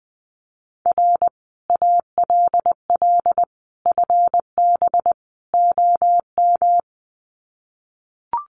Morse code, heard as a keyed tone, about 20 wpm, 700 Hz. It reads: R ALL FB OM
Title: Common QSO Phrases 20wpm